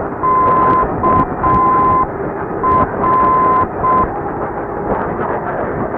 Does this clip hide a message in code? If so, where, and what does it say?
Yes: Morse code from 0.2 to 4.0 s, heard as a keyed tone, about 6 wpm, 1030 Hz. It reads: KR